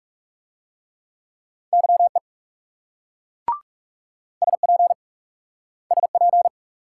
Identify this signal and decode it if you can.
Morse code, heard as a keyed tone, about 38 wpm, 700 Hz. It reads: YE SP SP